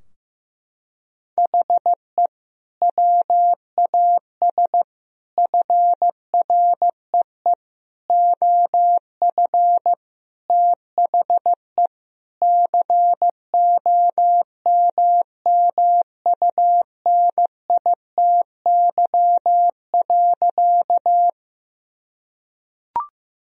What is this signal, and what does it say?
Morse code, heard as a keyed tone, about 16 wpm, 700 Hz. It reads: HE WAS FREE OF THE COMMUNITY.
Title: Sentences from Top 1000 Words - Review Part 4 - 15wpm